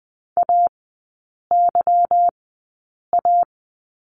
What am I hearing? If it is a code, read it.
Morse code, heard as a keyed tone, about 20 wpm, spaced as for 8 wpm, 700 Hz. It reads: AYA
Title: 145 Lesson 24e - Letter y - Three Characters 20wpm